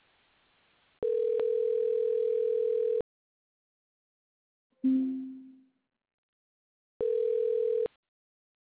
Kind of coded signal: telephone call progress tone